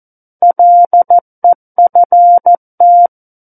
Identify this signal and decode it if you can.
Morse code, heard as a keyed tone, about 14 wpm, 700 Hz. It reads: LEFT
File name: Book One / 05-Chapter 5 - 14 WPM